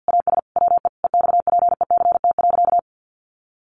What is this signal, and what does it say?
Morse code, heard as a keyed tone, about 50 wpm, 700 Hz. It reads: AS WE EXPECT.
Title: Sentences from Top 1100 Words 50wpm